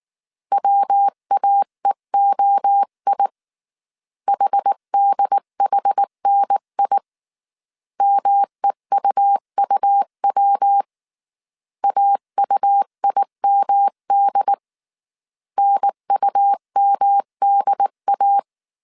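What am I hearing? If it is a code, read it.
Morse code, heard as a keyed tone, about 19 wpm, 800 Hz. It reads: WAEOI HDHNI MEUUW AUIMD NUMDA
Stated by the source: Morse code dictation at 18 wpm from the IW3SGT QRS Morse code training course, five-character groups of 300 characters.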